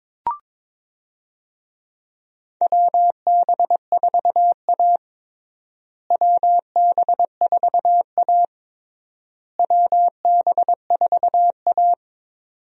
Morse code, heard as a keyed tone, about 22 wpm, 700 Hz. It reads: WB4A WB4A WB4A
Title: USA Call Signs - 4 Characters - Repeated 3x - 22wpm